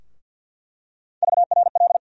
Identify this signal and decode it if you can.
Morse code, heard as a keyed tone, about 49 wpm, 700 Hz. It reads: VRL
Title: USA Call Sign Postfixes - 3 Letters 50wpm